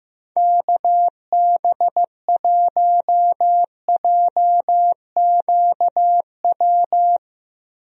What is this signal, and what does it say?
Morse code, heard as a keyed tone, about 15 wpm, 700 Hz. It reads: KB1JQW
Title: USA Call Signs - 6 Characters - Repeated 3x - 15wpm